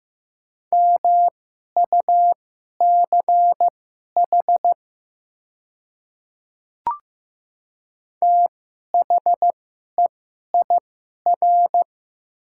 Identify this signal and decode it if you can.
Morse code, heard as a keyed tone, about 15 wpm, spaced as for 11 wpm, 690 Hz. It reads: MUCH THEIR
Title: 2x Character Spacing - Top 200 Words - 15wpm